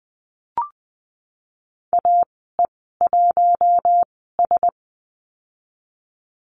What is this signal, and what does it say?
Morse code, heard as a keyed tone, about 20 wpm, spaced as for 15 wpm, 700 Hz. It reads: AE1S
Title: 055 Lesson 10g - Letter r - Call Signs 20wpm